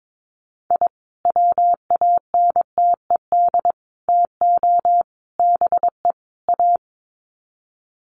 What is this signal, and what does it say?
Morse code, heard as a keyed tone, about 22 wpm, 700 Hz. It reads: I WANTED TO BE A
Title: Sets of 5 Words Encore2 22wpm